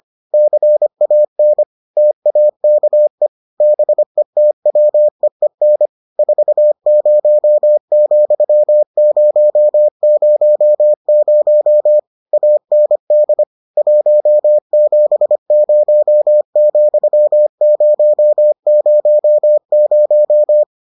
Morse code, heard as a keyed tone, about 25 wpm, 600 Hz. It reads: CAN TAKE BETWEEN 40,000 AND 170,000